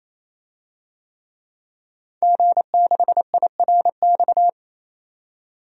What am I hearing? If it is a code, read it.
Morse code, heard as a keyed tone, about 28 wpm, 700 Hz. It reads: G6IRX
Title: DXCC-100 Call Signs - 5 Characters - 28wpm